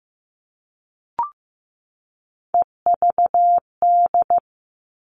Morse code, heard as a keyed tone, about 15 wpm, 700 Hz. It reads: EVD
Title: All Cut Numbers - 3-Digit 15wpm